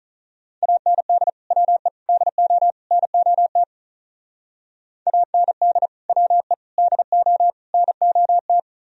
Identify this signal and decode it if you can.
Morse code, heard as a keyed tone, about 45 wpm, 700 Hz. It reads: AND WE DO NOT AND WE DO NOT